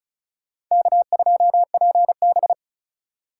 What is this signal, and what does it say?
Morse code, heard as a keyed tone, about 45 wpm, 700 Hz. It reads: K2PB